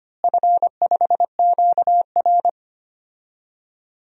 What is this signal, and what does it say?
Morse code, heard as a keyed tone, about 25 wpm, 700 Hz. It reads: F5QR
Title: DXCC-100 Call Signs - 4 Characters - Repeated 2x - 25wpm